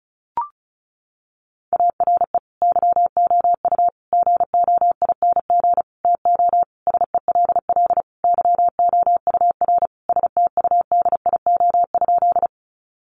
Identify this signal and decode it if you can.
Morse code, heard as a keyed tone, about 34 wpm, 700 Hz. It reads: ARE YOU GOING TO SELL YOUR STUDIO?
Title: Sentences from Top 1200 Words 35wpm